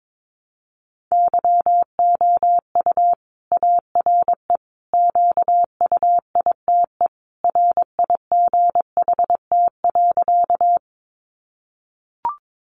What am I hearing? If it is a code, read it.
Morse code, heard as a keyed tone, about 22 wpm, 700 Hz. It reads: YOU ARE QUITE RIGHT.
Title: Sentences from Top 700 Words 22wpm